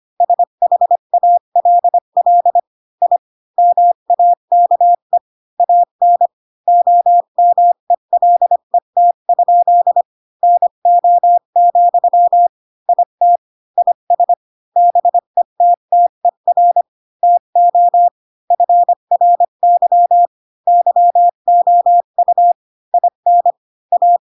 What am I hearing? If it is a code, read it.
Morse code, heard as a keyed tone, about 25 wpm, 700 Hz. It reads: SHALL I MAKE AN OMELET? NO, IT IS BETTER TO FRY YOU IN A
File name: Pinocchio / Pinocchio ch5 20 25 ik1hge